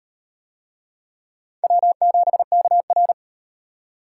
Morse code, heard as a keyed tone, about 38 wpm, 700 Hz. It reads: W7KR